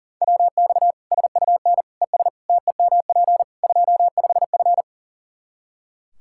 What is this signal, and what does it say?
Morse code, heard as a keyed tone, about 40 wpm, 700 Hz. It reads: WX SUN ES TEMP 25F
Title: Practice QSO - Exchange 2 - Info - 40wpm